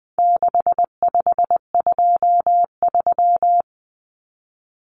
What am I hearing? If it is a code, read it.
Morse code, heard as a keyed tone, about 20 wpm, 700 Hz. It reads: 6523
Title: Numbers - 4-Digit 20wpm